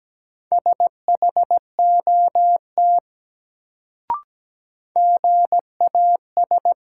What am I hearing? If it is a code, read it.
Morse code, heard as a keyed tone, about 17 wpm, 700 Hz. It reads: SHOT GAS
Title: Sentences from Top 1100 Words - New Word Drill Alt - 17wpm